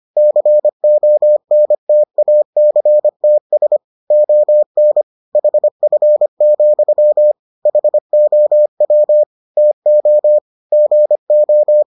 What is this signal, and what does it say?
Morse code, heard as a keyed tone, about 25 wpm, 600 Hz. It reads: CONTACTS ON HF, HOW TO GO